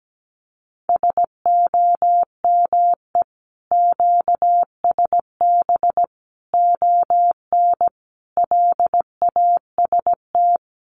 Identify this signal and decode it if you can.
Morse code, heard as a keyed tone, about 17 wpm, 700 Hz. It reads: SOME QSB ON LAST
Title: Common QSO Phrases 17wpm